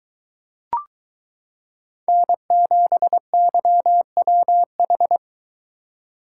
Morse code, heard as a keyed tone, about 23 wpm, 700 Hz. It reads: N7YWH